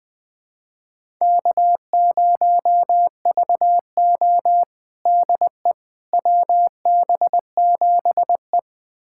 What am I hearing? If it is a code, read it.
Morse code, heard as a keyed tone, about 20 wpm, 700 Hz. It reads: K0VO DE WB7E
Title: USA Callsign DE Callsign - 4 Characters - 20wpm